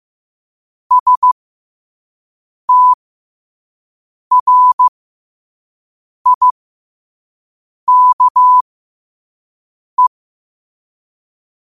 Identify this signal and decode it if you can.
Morse code, heard as a keyed tone, about 15 wpm, spaced as for 5 wpm, 1000 Hz. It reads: STRIKE
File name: qotdfiles / 5wpm